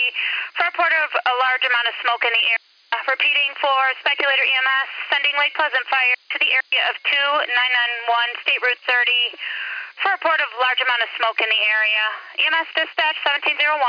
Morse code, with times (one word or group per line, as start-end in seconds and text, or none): none